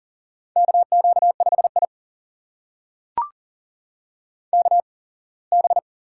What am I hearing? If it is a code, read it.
Morse code, heard as a keyed tone, about 40 wpm, spaced as for 24 wpm, 700 Hz. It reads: KQ5I KB
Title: USA Call Signs - 8x Character Spacing - 40wpm